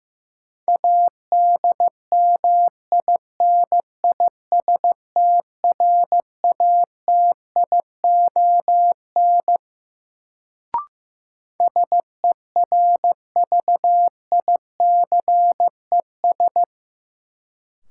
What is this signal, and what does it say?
Morse code, heard as a keyed tone, about 15 wpm, 700 Hz. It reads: ADMINISTRATION SERVICES
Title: Sets of 5 Words - New Word Drill - 15wpm